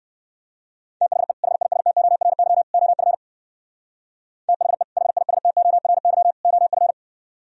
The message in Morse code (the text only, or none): THE HISTORY OF THE HISTORY OF